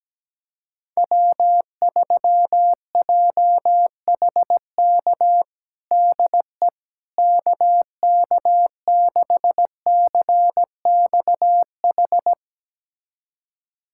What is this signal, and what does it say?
Morse code, heard as a keyed tone, about 17 wpm, 700 Hz. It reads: W3JHK DE KK6CXH
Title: USA Callsign DE Callsign 17wpm